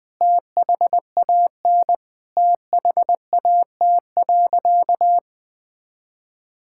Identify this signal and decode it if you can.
Morse code, heard as a keyed tone, about 20 wpm, 700 Hz. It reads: THAN THAT.
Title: Sentences with 2-Word Phrases - 20wpm